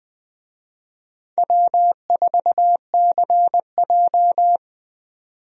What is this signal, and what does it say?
Morse code, heard as a keyed tone, about 20 wpm, 700 Hz. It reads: W4CJ
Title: USA Call Signs - 4 Characters - 20wpm